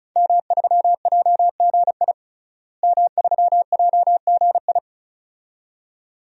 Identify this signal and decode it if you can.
Morse code, heard as a keyed tone, about 34 wpm, 700 Hz. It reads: M3JGI M3JGI
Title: DXCC-100 Call Signs - 5 Characters - Repeated 2x - 35wpm